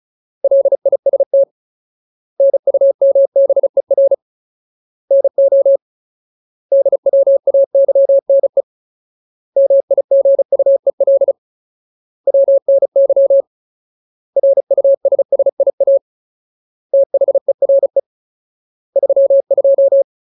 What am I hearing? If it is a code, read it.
Morse code, heard as a keyed tone, about 34 wpm, 550 Hz. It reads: LIST NUMBER NO DWAYNE MIGUEL WNY RUSSIA THERE 32
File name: Ultimate Word List / 35WPM Ultimate